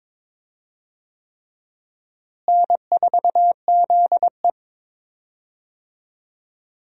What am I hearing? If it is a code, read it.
Morse code, heard as a keyed tone, about 22 wpm, 700 Hz. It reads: N4ZE
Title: USA Call Signs - 4 Characters - Repeated 3x - 22wpm